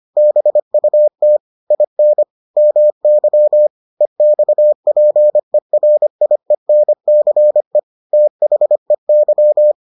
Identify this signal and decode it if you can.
Morse code, heard as a keyed tone, about 25 wpm, 600 Hz. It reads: BUT IN MY EXPERIENCE THEY